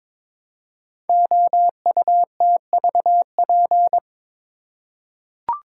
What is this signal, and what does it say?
Morse code, heard as a keyed tone, about 22 wpm, 700 Hz. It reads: OUTVP
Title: Code Groups - 5 Letter:Numbers - 22wpm